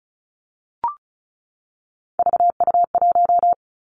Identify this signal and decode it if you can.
Morse code, heard as a keyed tone, about 36 wpm, 700 Hz. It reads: VU1
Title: Three Challenging Letter-Number Combinations 35wpm